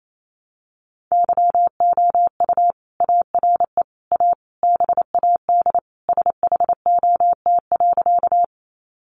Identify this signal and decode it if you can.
Morse code, heard as a keyed tone, about 28 wpm, 700 Hz. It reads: YOU ARE A BAD SHOT.
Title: Sentences from Top 1100 Words 28wpm